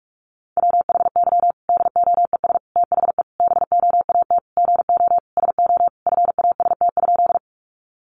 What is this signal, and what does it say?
Morse code, heard as a keyed tone, about 44 wpm, 700 Hz. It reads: WHY DOES THE BOAT GO SO FAST?
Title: Sentences from Top 500 Words 45wpm